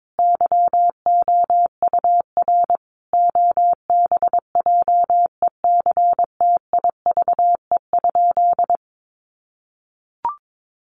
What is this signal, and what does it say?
Morse code, heard as a keyed tone, about 22 wpm, 700 Hz. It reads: YOUR OBJECTIVE?
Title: Sentences from Top 1200 Words 22wpm